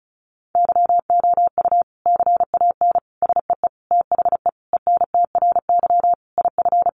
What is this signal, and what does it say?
Morse code, heard as a keyed tone, about 34 wpm, 700 Hz. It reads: YOU CAN SEE THE ENTRY IF
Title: Sentences from Top 1500 Words 35wpm